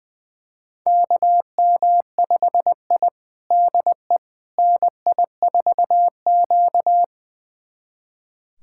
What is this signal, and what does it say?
Morse code, heard as a keyed tone, about 20 wpm, 700 Hz. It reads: KM5I DE NI4Q
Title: USA Callsign DE Callsign - 4 Characters - 20wpm